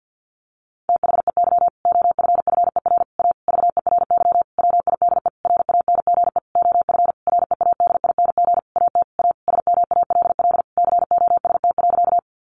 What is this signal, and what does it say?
Morse code, heard as a keyed tone, about 48 wpm, 700 Hz. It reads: THEY OFFER A VERY WIDE RANGE OF READING AT A SMALL COST.
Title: Sentences from Top 900 Words 50wpm